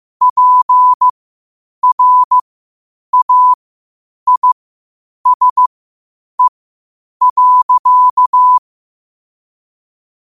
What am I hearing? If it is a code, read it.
Morse code, heard as a keyed tone, about 15 wpm, spaced as for 8 wpm, 1000 Hz. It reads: PRAISE.
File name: qotdfiles / 8wpm